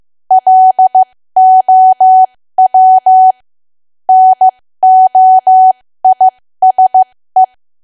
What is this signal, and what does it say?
Morse code, heard as a keyed tone, about 15 wpm, spaced as for 13 wpm, 750 Hz. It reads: LOW NOISE
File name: cwa / W1AW-Level2 / 13WPM